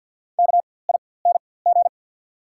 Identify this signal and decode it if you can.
Morse code, heard as a keyed tone, about 53 wpm, spaced as for 24 wpm, 690 Hz. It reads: KING
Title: 4x Character Spacing - Top 500 Words - 50wpm